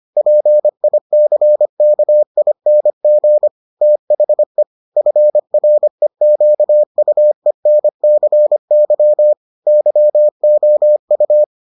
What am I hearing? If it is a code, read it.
Morse code, heard as a keyed tone, about 25 wpm, 600 Hz. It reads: PICKING THE FREQUENCY YOU